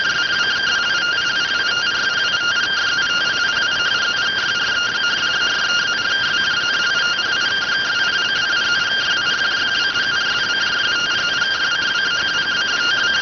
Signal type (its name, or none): two-tone teleprinter signal